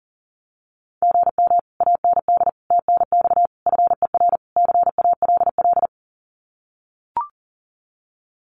Morse code, heard as a keyed tone, about 40 wpm, 700 Hz. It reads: GM AND TNX FER CALL
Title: Common QSO Phrases Encore 40wpm